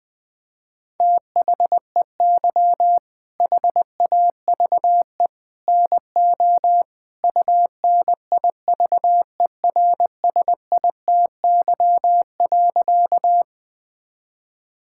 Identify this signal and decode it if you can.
Morse code, heard as a keyed tone, about 20 wpm, 700 Hz. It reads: THEY HAVE NO UNIVERSITY.
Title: Sentences from Top 1000 Words - Review Part 5 - 20wpm